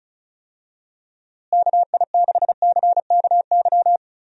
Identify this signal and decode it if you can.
Morse code, heard as a keyed tone, about 34 wpm, 700 Hz. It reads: KI6CKY
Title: USA Call Signs - 6 Characters - Repeated 3x - 35wpm